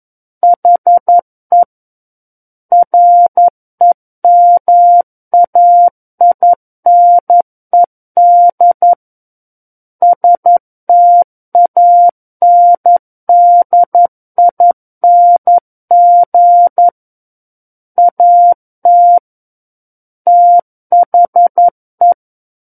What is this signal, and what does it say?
Morse code, heard as a keyed tone, about 11 wpm, 700 Hz. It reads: HE REMAINED STANDING AT THE
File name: 02-Chapter 2 - 11 WPM